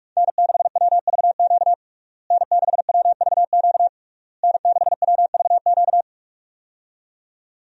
Morse code, heard as a keyed tone, about 46 wpm, 700 Hz. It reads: N6WVQ N6WVQ N6WVQ
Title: USA Call Signs - 5 Characters - Repeated 3x - 45wpm